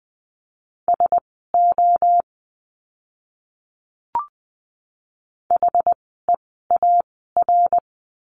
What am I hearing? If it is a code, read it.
Morse code, heard as a keyed tone, about 20 wpm, spaced as for 14 wpm, 700 Hz. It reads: SO HEAR
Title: 230 Lesson 37f - Letter x - Words 20wpm